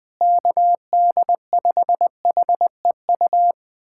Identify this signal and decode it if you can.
Morse code, heard as a keyed tone, about 20 wpm, 700 Hz. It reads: KD5HEU
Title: USA Callsign DE Callsign - 6 Characters - 20wpm